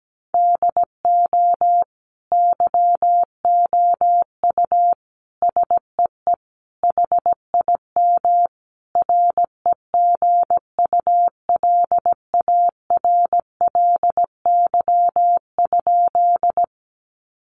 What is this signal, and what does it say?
Morse code, heard as a keyed tone, about 17 wpm, 700 Hz. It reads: DO YOU SEE HIM REGULARLY?
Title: Sentences from Top 1400 Words 17wpm